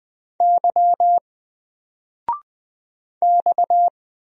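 Morse code, heard as a keyed tone, about 20 wpm, 700 Hz. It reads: Y X
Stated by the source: Single letter and number Morse drill at 20 wpm, from N4LCD.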